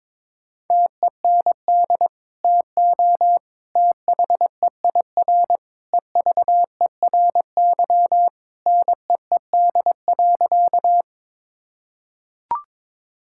Morse code, heard as a keyed tone, about 22 wpm, 700 Hz. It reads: TEND TO THEIR EVERY NEED.
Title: Sentences from Top 1000 Words - Review Part 5 - 22wpm